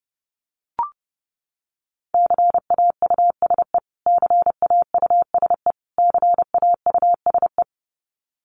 Morse code, heard as a keyed tone, about 30 wpm, 700 Hz. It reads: CAUSE CAUSE CAUSE
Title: Top 200 Words Repeated 30wpm